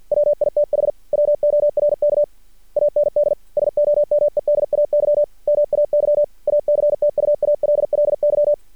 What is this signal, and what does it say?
Morse code, heard as a keyed tone, about 50 wpm, 590 Hz. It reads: WITH WORK AND SOMEDAY MAY ACTUALLY
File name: archived / CRSnet